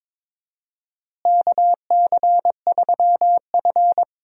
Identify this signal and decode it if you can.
Morse code, heard as a keyed tone, about 22 wpm, 700 Hz. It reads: KC3F
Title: USA Call Signs - 4 Characters - Repeated 3x - 22wpm